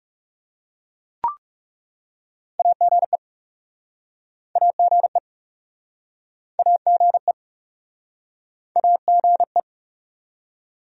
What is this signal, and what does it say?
Morse code, heard as a keyed tone, about 41 wpm, 700 Hz. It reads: AGE AGE AGE AGE